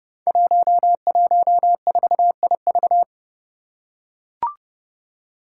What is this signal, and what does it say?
Morse code, heard as a keyed tone, about 30 wpm, 700 Hz. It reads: 114IV
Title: Five Challenging Letter-Number Combinations 30wpm